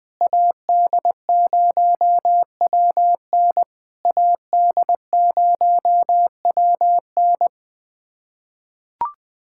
Morse code, heard as a keyed tone, about 20 wpm, 700 Hz. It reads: AD0WN AD0WN
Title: POTA QSO Practice - 20wpm